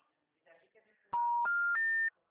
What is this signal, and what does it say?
Special information tone, with speech.